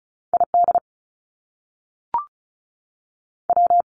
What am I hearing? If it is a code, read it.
Morse code, heard as a keyed tone, about 36 wpm, 700 Hz. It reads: ID W
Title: US States 35wpm